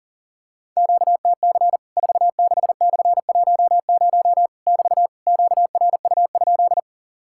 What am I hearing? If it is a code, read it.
Morse code, heard as a keyed tone, about 40 wpm, 700 Hz. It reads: QTC 46/10 <BT> QRU?
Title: Worked All Europe DX Contest - QTC Exchange - 40wpm